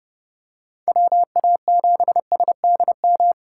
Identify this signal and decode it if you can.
Morse code, heard as a keyed tone, about 30 wpm, 700 Hz. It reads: WA7SDM
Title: USA Call Signs - 6 Characters - 6x Character Spacing - 30wpm